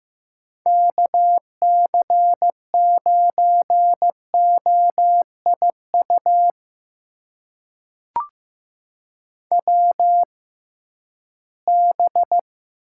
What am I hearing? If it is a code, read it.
Morse code, heard as a keyed tone, about 15 wpm, spaced as for 11 wpm, 700 Hz. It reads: KC9OIU WB
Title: USA Call Signs - 6 Characters - 6x Character Spacing - 15wpm